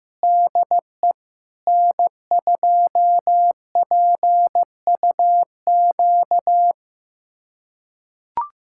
Morse code, heard as a keyed tone, about 15 wpm, 700 Hz. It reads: DE N2PUQ
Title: USA Callsign DE Callsign 15wpm